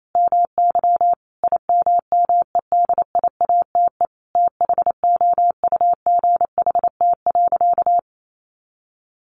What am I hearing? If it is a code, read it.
Morse code, heard as a keyed tone, about 28 wpm, 700 Hz. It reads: MY IMMEDIATE THOUGHT.
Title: Sentences from Top 1500 Words 28wpm